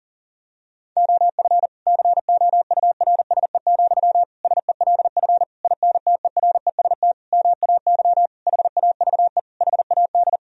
Morse code, heard as a keyed tone, about 40 wpm, 700 Hz. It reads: OF COURSE, SELF INTEREST MAY HAVE HAD
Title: Sentences from Top 1000 Words - Review Part 2 - 40wpm